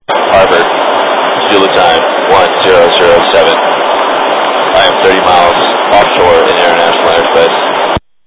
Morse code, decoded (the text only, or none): none